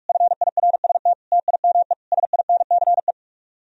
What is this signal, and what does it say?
Morse code, heard as a keyed tone, about 58 wpm, 700 Hz. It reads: FIRST TIME SINCE